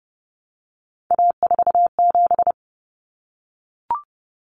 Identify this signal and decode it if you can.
Morse code, heard as a keyed tone, about 30 wpm, 700 Hz. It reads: A47
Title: Common Cut Numbers - 3-Digit 30wpm